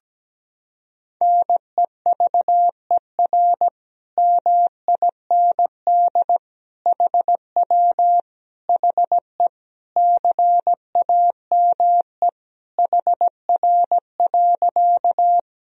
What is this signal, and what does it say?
Morse code, heard as a keyed tone, about 17 wpm, 700 Hz. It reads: NEVER MIND HW HE CAME HR.
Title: Sentences from Top 500 Words and Abbreviations 17wpm